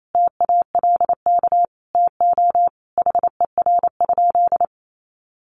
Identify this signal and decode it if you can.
Morse code, heard as a keyed tone, about 28 wpm, 700 Hz. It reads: TALK TO HER?
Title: Sentences from Top 1400 Words 28wpm